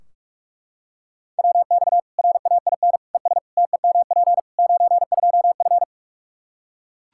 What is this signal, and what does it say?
Morse code, heard as a keyed tone, about 41 wpm, 700 Hz. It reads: WX RAIN ES TEMP 92F
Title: Practice QSO - Exchange 2 - Info - 45wpm